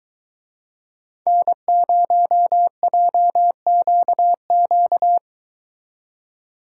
Morse code, heard as a keyed tone, about 22 wpm, 700 Hz. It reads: N0JQQ